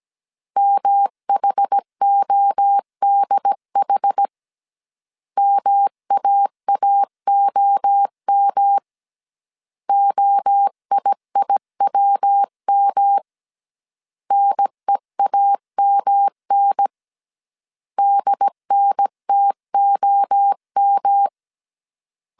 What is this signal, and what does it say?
Morse code, heard as a keyed tone, about 17 wpm, 800 Hz. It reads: MHODH MAAOM OIIWM NEAMN DNTOM
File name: CW / DET / 1 3 1500C S C 16wpm 2148